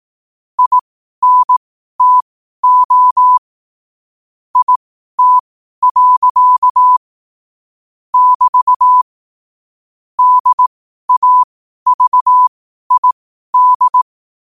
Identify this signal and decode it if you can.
Morse code, heard as a keyed tone, about 18 wpm, spaced as for 12 wpm, 1000 Hz. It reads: INTO IT. <BT> DAVID